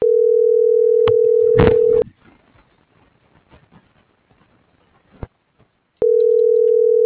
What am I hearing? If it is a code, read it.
Ringback tone (North America), 440 + 480 Hz.